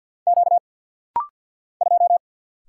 Morse code, heard as a keyed tone, about 50 wpm, 700 Hz. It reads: Q 2
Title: Single Letter-Number Alt - Rapid-Fire 50wpm